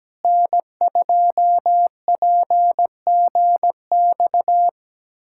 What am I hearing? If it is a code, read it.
Morse code, heard as a keyed tone, about 17 wpm, 700 Hz. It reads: N2PGX
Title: USA Callsign DE Callsign - 5 Characters - 17wpm